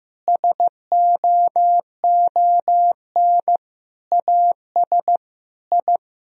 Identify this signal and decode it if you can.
Morse code, heard as a keyed tone, about 15 wpm, 700 Hz. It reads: SOON AS I
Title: Sentences from Top 1000 Words - Review Part 3 - 15wpm